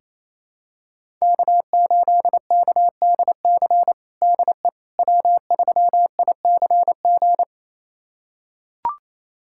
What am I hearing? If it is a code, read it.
Morse code, heard as a keyed tone, about 28 wpm, 700 Hz. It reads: K8KDC DE W3ICG
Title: USA Callsign DE Callsign - 5 Characters - 28wpm